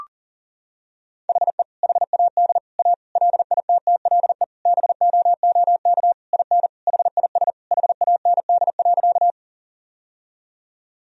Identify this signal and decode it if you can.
Morse code, heard as a keyed tone, about 39 wpm, 700 Hz. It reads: HE HAD A LITTLE BOOK IN HIS HAND.
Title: Sentences from Top 1000 Words - Review Part 1 - 40wpm